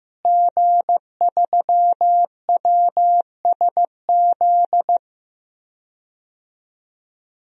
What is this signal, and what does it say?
Morse code, heard as a keyed tone, about 15 wpm, 700 Hz. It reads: G3WSZ
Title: DXCC-100 Call Signs - 5 Characters - Repeated 3x - 15wpm